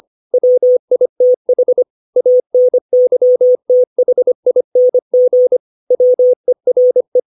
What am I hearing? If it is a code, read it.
Morse code, heard as a keyed tone, about 25 wpm, 500 Hz. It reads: WITH ANYTHING WERE
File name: 632 words / 632 Words 25WPM